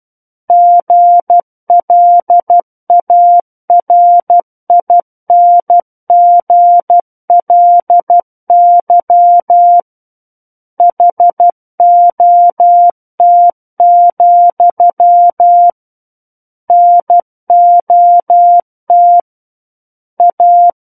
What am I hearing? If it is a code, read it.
Morse code, heard as a keyed tone, about 12 wpm, 700 Hz. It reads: GLARINGLY HOT, NOT A
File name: Book One / 03-Chapter 3 - 12 WPM